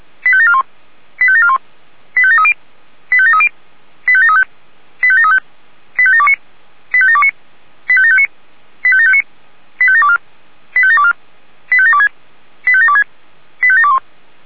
Five-tone selective call (ZVEI), heard as quick runs of five tones, tones of about 70 ms. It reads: 85741 85741 85730 85730 85735 85735 85729 85729 85759 85759 85724 85724 85726 85726 85721